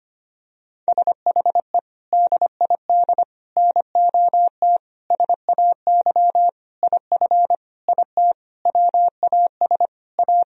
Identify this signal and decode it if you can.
Morse code, heard as a keyed tone, about 25 wpm, 700 Hz. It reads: SHE DID NOT SAY IF IT WAS A
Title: Sentences from Top 1000 Words - Review Part 3 - 25wpm